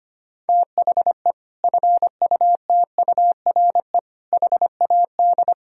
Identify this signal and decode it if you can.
Morse code, heard as a keyed tone, about 25 wpm, 700 Hz. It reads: THE FUTURE HAD
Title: Sentences from Top 1000 Words - Review Part 4 - 25wpm